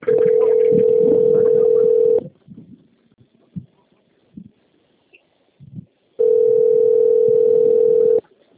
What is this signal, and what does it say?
Ringback tone (North America), 440 + 480 Hz.
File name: out- - - - - .64519